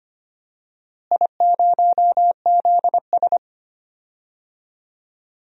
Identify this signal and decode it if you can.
Morse code, heard as a keyed tone, about 25 wpm, 700 Hz. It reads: I0ZS
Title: DXCC-100 Call Signs - 4 Characters - Repeated 3x - 25wpm